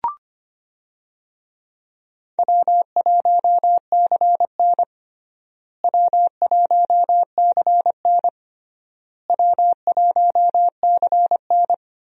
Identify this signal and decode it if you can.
Morse code, heard as a keyed tone, about 25 wpm, 700 Hz. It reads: W1CN W1CN W1CN
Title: USA Call Signs - 4 Characters - Repeated 3x - 25wpm